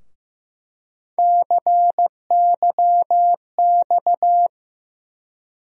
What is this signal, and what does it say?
Morse code, heard as a keyed tone, about 15 wpm, 700 Hz. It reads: CYX
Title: USA Call Sign Postfixes - 3 Letters 15wpm